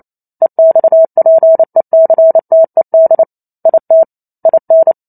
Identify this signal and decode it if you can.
Morse code, heard as a keyed tone, about 28 wpm, 650 Hz. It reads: EXPECTED IT IN
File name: Chapter 2 - 28 WPM